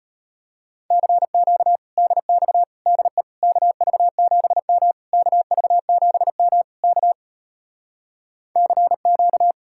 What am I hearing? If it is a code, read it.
Morse code, heard as a keyed tone, about 33 wpm, 700 Hz. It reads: CQ DX DE KV7M KV7M K CQ